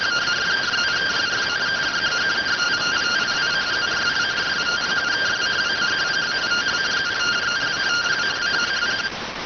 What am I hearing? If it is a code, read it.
Two-tone teleprinter signal, about 170 Hz shift.